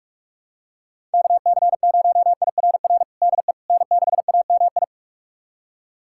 Morse code, heard as a keyed tone, about 46 wpm, 700 Hz. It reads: KC0IRR DE N6AMI
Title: USA Callsign DE Callsign 45wpm